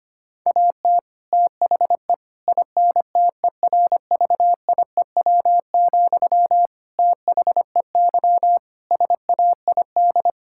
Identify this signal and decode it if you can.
Morse code, heard as a keyed tone, about 25 wpm, 700 Hz. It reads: AT THE INTERVIEW, THEY SAID
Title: Sentences from Top 1600 Words 25wpm